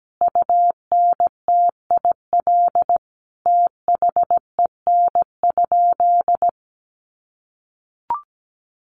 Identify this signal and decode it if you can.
Morse code, heard as a keyed tone, about 17 wpm, 700 Hz. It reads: UNTIL THEN?
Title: Sentences from Top 500 Words 17wpm